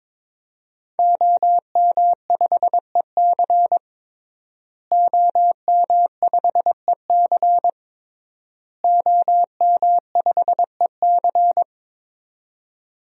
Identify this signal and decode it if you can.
Morse code, heard as a keyed tone, about 22 wpm, 700 Hz. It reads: OM5EC OM5EC OM5EC
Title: DXCC-100 Call Signs - 5 Characters - Repeated 3x - 22wpm